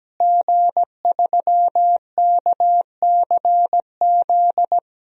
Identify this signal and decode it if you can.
Morse code, heard as a keyed tone, about 17 wpm, 700 Hz. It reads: G3KCZ
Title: DXCC-100 Call Signs - 5 Characters - Repeated 2x - 17wpm